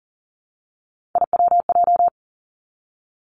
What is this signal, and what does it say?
Morse code, heard as a keyed tone, about 39 wpm, 690 Hz. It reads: IWJ
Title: Three Challenging Letter-Number Combinations 40wpm